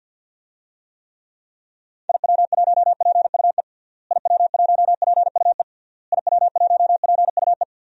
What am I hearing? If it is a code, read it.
Morse code, heard as a keyed tone, about 49 wpm, 700 Hz. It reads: IW1PUE IW1PUE IW1PUE
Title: DXCC-100 Call Signs - 6 Characters - Repeated 3x - 50wpm